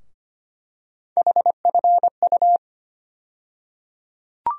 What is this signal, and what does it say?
Morse code, heard as a keyed tone, about 25 wpm, 700 Hz. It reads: HFU
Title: USA Call Sign Postfixes - 3 Letters 25wpm